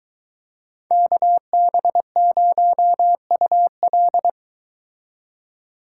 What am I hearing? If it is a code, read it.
Morse code, heard as a keyed tone, about 22 wpm, 700 Hz. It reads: KB0UL